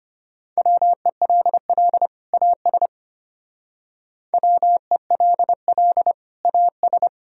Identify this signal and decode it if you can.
Morse code, heard as a keyed tone, about 32 wpm, 700 Hz. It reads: WELL AS WELL AS